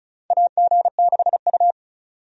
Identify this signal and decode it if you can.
Morse code, heard as a keyed tone, about 36 wpm, 700 Hz. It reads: AG6U
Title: USA Call Signs - 4 Characters - 35wpm